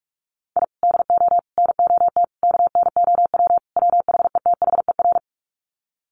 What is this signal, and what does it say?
Morse code, heard as a keyed tone, about 44 wpm, 700 Hz. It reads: I DO NOT KNOW WHETHER
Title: Sets of 5 Words 45wpm